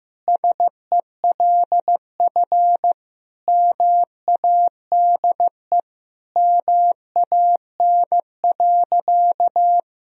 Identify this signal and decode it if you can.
Morse code, heard as a keyed tone, about 15 wpm, 700 Hz. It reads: SELF MADE MAN.
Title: Sentences from Top 1000 Words - Review Part 1 - 15wpm